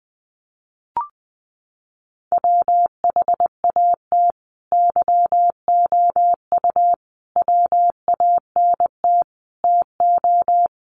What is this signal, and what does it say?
Morse code, heard as a keyed tone, about 20 wpm, 700 Hz. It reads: WHAT YOU WANT TO
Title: Sets of 5 Words Encore 20wpm